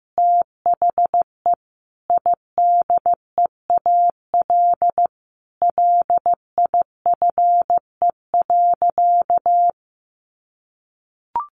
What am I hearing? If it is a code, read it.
Morse code, heard as a keyed tone, about 15 wpm, 700 Hz. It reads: THE IDEAL LIFE.
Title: Sentences from Top 1400 Words 15wpm